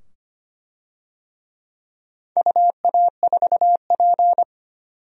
Morse code, heard as a keyed tone, about 25 wpm, 700 Hz. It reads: UA4P
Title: DXCC-100 Call Signs - 4 Characters - 25wpm